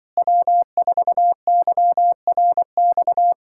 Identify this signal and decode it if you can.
Morse code, heard as a keyed tone, about 25 wpm, 700 Hz. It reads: W4YRX